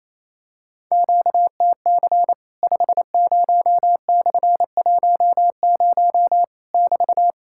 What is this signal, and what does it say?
Morse code, heard as a keyed tone, about 28 wpm, 700 Hz. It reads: QTC 50/10 <BT>
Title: Worked All Europe DX Contest - QTC Exchange - 28wpm